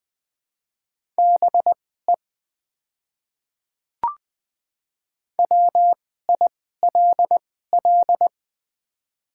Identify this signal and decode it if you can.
Morse code, heard as a keyed tone, about 20 wpm, spaced as for 14 wpm, 700 Hz. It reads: BE WILL
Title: 2x Character Spacing - Top 200 Words - 20wpm